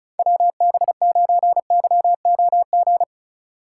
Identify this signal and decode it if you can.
Morse code, heard as a keyed tone, about 34 wpm, 700 Hz. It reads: WB9YOG
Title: USA Call Signs - 6 Characters - 2x Character Spacing - 35wpm